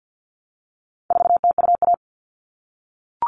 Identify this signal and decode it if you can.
Morse code, heard as a keyed tone, about 50 wpm, 700 Hz. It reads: 4TUA
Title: All Cut Numbers - 4-Digit 50wpm